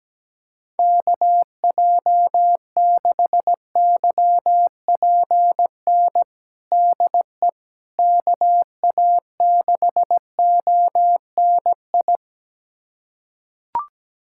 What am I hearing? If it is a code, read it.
Morse code, heard as a keyed tone, about 17 wpm, 700 Hz. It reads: KJ6YPN DE KA6ONI
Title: USA Callsign DE Callsign - 6 Characters - 17wpm